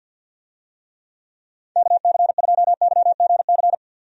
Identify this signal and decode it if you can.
Morse code, heard as a keyed tone, about 46 wpm, 700 Hz. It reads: KC2YGC
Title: USA Call Signs - 6 Characters - 50wpm